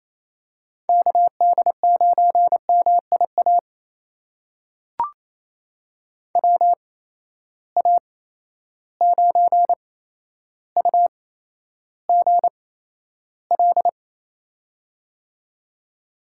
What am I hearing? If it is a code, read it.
Morse code, heard as a keyed tone, about 28 wpm, spaced as for 12 wpm, 700 Hz. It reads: KD9MIA WA9UGL
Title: USA Call Signs - 6 Characters - 8x Character Spacing - 28wpm